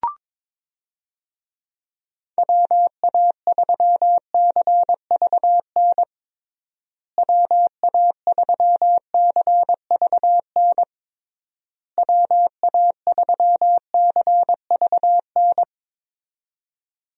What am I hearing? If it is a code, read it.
Morse code, heard as a keyed tone, about 22 wpm, 700 Hz. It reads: WA3CVN WA3CVN WA3CVN
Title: USA Call Signs - 6 Characters - Repeated 3x - 22wpm